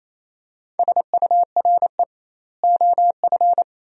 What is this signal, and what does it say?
Morse code, heard as a keyed tone, about 28 wpm, 700 Hz. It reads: SURE OF
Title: 2x Word-Spacing - Sentences from Top 300 Words - 28wpm